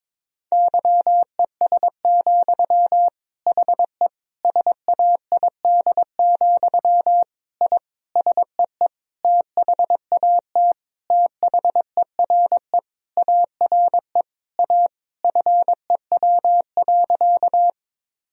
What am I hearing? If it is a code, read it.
Morse code, heard as a keyed tone, about 22 wpm, 700 Hz. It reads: YES, HE SAID, I SEE THAT THERE ARE A FEW.
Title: Sentences from Top 1000 Words - Review Part 1 - 22wpm